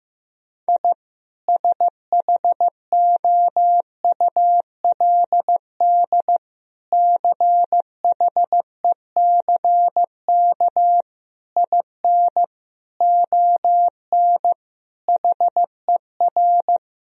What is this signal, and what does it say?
Morse code, heard as a keyed tone, about 15 wpm, 700 Hz. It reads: I SHOULD CHECK IN ON HER
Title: Sentences from Top 1000 Words - Review Part 3 - 15wpm